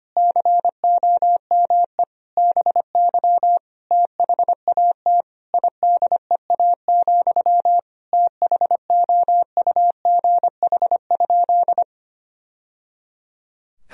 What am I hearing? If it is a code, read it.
Morse code, heard as a keyed tone, about 25 wpm, 700 Hz. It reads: COME BY THAT IDEA, THOUGH?
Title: Sentences from Top 1000 Words - Review Part 2 - 25wpm